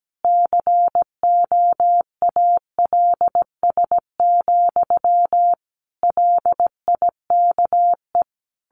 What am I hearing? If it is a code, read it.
Morse code, heard as a keyed tone, about 17 wpm, 700 Hz. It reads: COALS, LIKE
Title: A Sound Of Thunder 17wpm